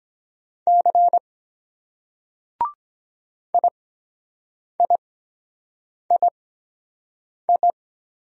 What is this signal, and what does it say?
Morse code, heard as a keyed tone, about 23 wpm, spaced as for 7 wpm, 700 Hz. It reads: C IIII